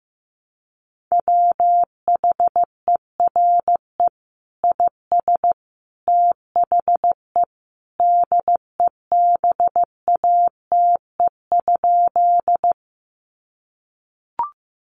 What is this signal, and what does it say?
Morse code, heard as a keyed tone, about 15 wpm, 700 Hz. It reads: WHERE IS THE DEBATE?
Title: Sentences from Top 1100 Words 15wpm